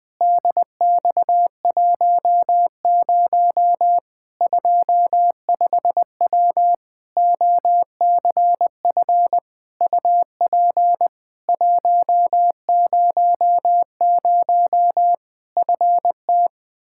Morse code, heard as a keyed tone, about 20 wpm, 700 Hz. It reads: DX10 25W OCF UP 100 FT
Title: Practice QSO - Exchange 2 - Info - 20wpm